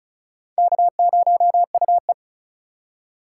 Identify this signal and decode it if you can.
Morse code, heard as a keyed tone, about 34 wpm, 700 Hz. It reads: K0UE
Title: USA Call Signs - 4 Characters - 2x Character Spacing - 35wpm